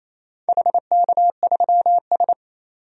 Morse code, heard as a keyed tone, about 27 wpm, 690 Hz. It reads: HK3S
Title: DXCC-100 Call Signs - 4 Characters - 28wpm